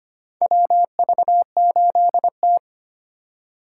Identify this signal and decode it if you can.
Morse code, heard as a keyed tone, about 25 wpm, 700 Hz. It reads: WV8T
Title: USA Call Signs - 4 Characters - 6x Character Spacing - 25wpm